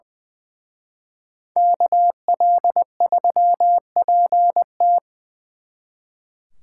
Morse code, heard as a keyed tone, about 23 wpm, 700 Hz. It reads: KL3PT
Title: Speed Racing - USA 2x2 Call Signs 20wpm